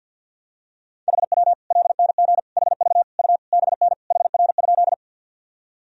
Morse code, heard as a keyed tone, about 46 wpm, 700 Hz. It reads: HW LNG HV U BN HR?
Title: Sentences from Top 200 Words and Abbreviations 50wpm